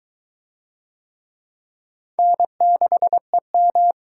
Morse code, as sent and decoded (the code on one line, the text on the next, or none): -. -.... --
N6EM